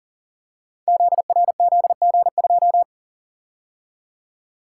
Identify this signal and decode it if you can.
Morse code, heard as a keyed tone, about 42 wpm, 690 Hz. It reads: ZRZG2
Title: Code Groups - 5 Letter:Numbers - 40wpm